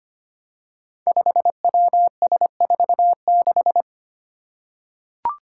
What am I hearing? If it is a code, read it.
Morse code, heard as a keyed tone, about 25 wpm, 700 Hz. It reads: 5WS46
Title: Five Challenging Letter-Number Combinations 25wpm